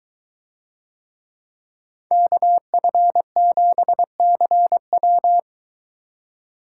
Morse code, heard as a keyed tone, about 22 wpm, 700 Hz. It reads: KF7CW